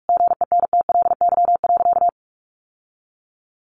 Morse code, heard as a keyed tone, about 44 wpm, 700 Hz. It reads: GENTLY.
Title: Sentences from Top 1500 Words 45wpm